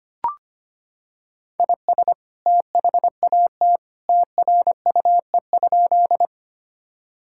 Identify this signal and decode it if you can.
Morse code, heard as a keyed tone, about 25 wpm, 700 Hz. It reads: IS THAT TRUE?
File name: practice / 25wpm / Sentences with 2-Word Phrases - 25wpm